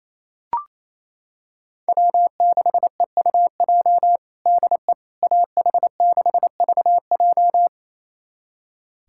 Morse code, heard as a keyed tone, about 28 wpm, 700 Hz. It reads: W6EUJ DE AH6VJ
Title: USA Callsign DE Callsign - 5 Characters - 28wpm